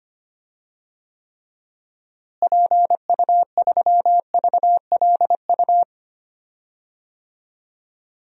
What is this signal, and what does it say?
Morse code, heard as a keyed tone, about 25 wpm, 700 Hz. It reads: PU3VLU
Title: DXCC-100 Call Signs - 6 Characters - Repeated 2x - 25wpm